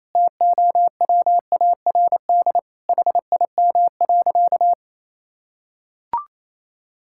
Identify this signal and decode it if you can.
Morse code, heard as a keyed tone, about 28 wpm, 690 Hz. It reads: TOWARD HIM.
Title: Sentences from Top 1000 Words - Review Part 6 - 28wpm